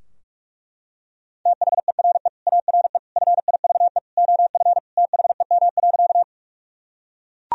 Morse code, heard as a keyed tone, about 41 wpm, 700 Hz. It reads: THERE ARE FIVE OF THEM.
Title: Sentences from Top 1000 Words - Review Part 2 - 45wpm